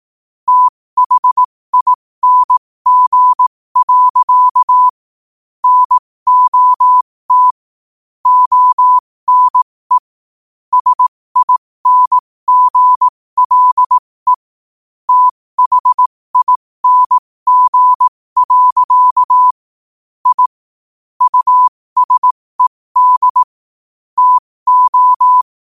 Morse code, heard as a keyed tone, about 18 wpm, spaced as for 15 wpm, 1000 Hz. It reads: THING. NOT ONE SINGLE THING. I USED TO
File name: qotdfiles / 15wpm